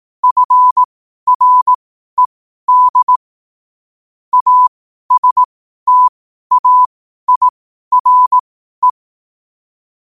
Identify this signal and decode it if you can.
Morse code, heard as a keyed tone, about 18 wpm, spaced as for 12 wpm, 1000 Hz. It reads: FRED ASTAIRE
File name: qotdfiles / 12wpm